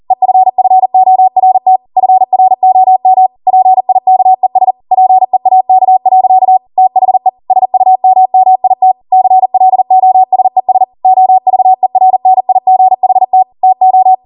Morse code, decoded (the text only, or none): EFFORT FROM PIKES PEAK. THE SUMMIT CLOSES OVERNIGHT TO